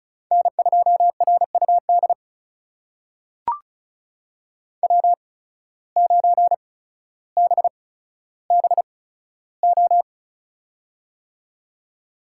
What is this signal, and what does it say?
Morse code, heard as a keyed tone, about 36 wpm, spaced as for 15 wpm, 700 Hz. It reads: N2RUD W9BBO
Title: USA Call Signs - 5 Characters - 8x Character Spacing - 35wpm